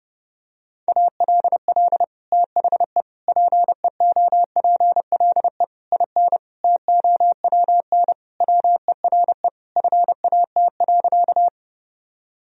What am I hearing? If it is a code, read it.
Morse code, heard as a keyed tone, about 30 wpm, 700 Hz. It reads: ALL THE PEOPLE IN TOWN WERE FAT.
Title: Sentences from Top 1000 Words - Review Part 4 - 30wpm